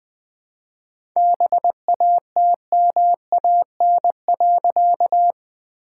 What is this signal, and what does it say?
Morse code, heard as a keyed tone, about 20 wpm, 700 Hz. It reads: BATMAN.
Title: All-Age Appropriate Jokes - 20wpm